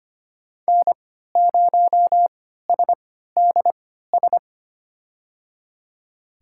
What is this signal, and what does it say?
Morse code, heard as a keyed tone, about 25 wpm, spaced as for 14 wpm, 700 Hz. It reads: N0SDS
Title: USA Call Signs - 5 Characters - 3x Character Spacing - 25wpm